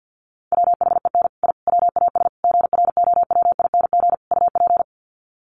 Morse code, heard as a keyed tone, about 48 wpm, 700 Hz. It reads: WHEN I WAS GROWING UP
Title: Sets of 5 Words Encore 50wpm